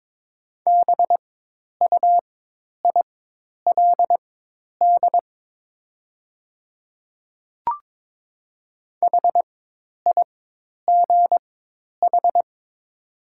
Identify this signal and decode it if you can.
Morse code, heard as a keyed tone, about 23 wpm, spaced as for 10 wpm, 700 Hz. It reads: BUILD HIGH
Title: 4x Character Spacing - Top 200 Words - 22wpm